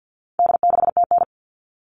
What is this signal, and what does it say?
Morse code, heard as a keyed tone, about 50 wpm, 700 Hz. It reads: D6TN
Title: All Cut Numbers - 4-Digit 50wpm